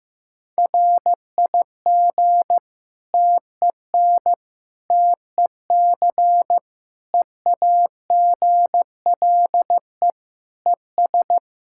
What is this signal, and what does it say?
Morse code, heard as a keyed tone, about 15 wpm, 700 Hz. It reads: RIG TEN TEC EAGLE ES
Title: K1USN SST QSO Practice 15wpm